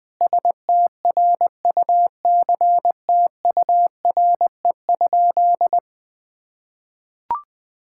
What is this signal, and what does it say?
Morse code, heard as a keyed tone, about 21 wpm, 700 Hz. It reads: STRUCTURE?
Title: Sentences from Top 1000 Words - Review Part 6 - 20wpm